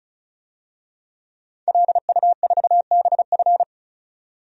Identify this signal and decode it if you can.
Morse code, heard as a keyed tone, about 34 wpm, 700 Hz. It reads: LU4BF
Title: DXCC-100 Call Signs - 5 Characters - 35wpm